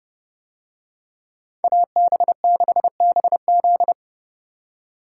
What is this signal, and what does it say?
Morse code, heard as a keyed tone, about 30 wpm, 700 Hz. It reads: AB6BZ